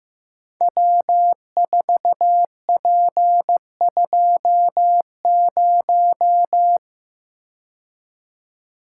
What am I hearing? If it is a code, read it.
Morse code, heard as a keyed tone, about 15 wpm, 700 Hz. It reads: W4P20
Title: Code Groups - 5 Letter:Numbers - 15wpm